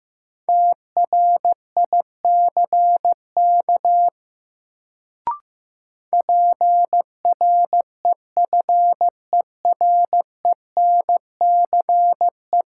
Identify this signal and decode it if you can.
Morse code, heard as a keyed tone, about 15 wpm, 700 Hz. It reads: TRICK PREFERENCE
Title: Sentences from Top 1500 Words - New Word Drill Alt - 15wpm